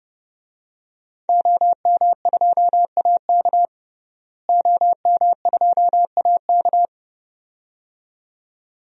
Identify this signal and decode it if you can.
Morse code, heard as a keyed tone, about 32 wpm, 700 Hz. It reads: OM2AK OM2AK
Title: DXCC-100 Call Signs - 5 Characters - Repeated 2x - 30wpm